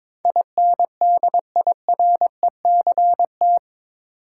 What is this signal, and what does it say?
Morse code, heard as a keyed tone, about 22 wpm, 700 Hz. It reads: INDIRECT
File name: practice / 22wpm / Eight Letter Words - 22wpm